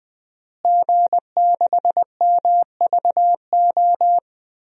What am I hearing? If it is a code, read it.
Morse code, heard as a keyed tone, about 20 wpm, 700 Hz. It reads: G6MVO
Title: DXCC-100 Call Signs - 5 Characters - Repeated 2x - 20wpm